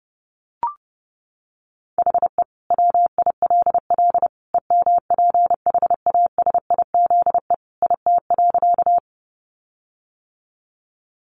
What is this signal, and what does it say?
Morse code, heard as a keyed tone, about 30 wpm, 700 Hz. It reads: HE WILL EMPHASIZE IT.
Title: Sentences from Top 1500 Words 30wpm